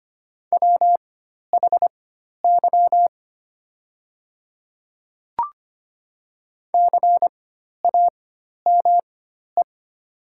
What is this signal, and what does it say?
Morse code, heard as a keyed tone, about 25 wpm, spaced as for 12 wpm, 700 Hz. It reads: WHY CAME
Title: 4x Character Spacing - Top 300 Words - 25wpm